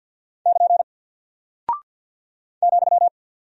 Morse code, heard as a keyed tone, about 50 wpm, 700 Hz. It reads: <KN> ,
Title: Single Character (Letter:Number:Punctuation:Prosign) Alt 50wpm